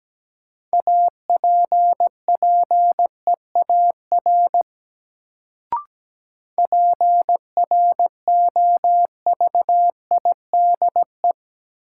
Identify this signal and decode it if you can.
Morse code, heard as a keyed tone, about 17 wpm, 700 Hz. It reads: APPEAR PROVIDE
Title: Sentences from Top 1000 Words - Review Part 2 - New Word Drill - 17wpm